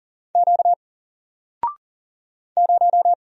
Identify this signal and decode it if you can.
Morse code, heard as a keyed tone, about 40 wpm, 700 Hz. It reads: Q 0
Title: Single Letter-Number Alt 40wpm